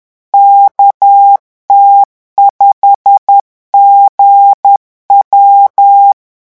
Morse code, heard as a keyed tone, about 10 wpm, 800 Hz. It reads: KT5GW
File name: morse mp3 / revision13-vitesse 10 mots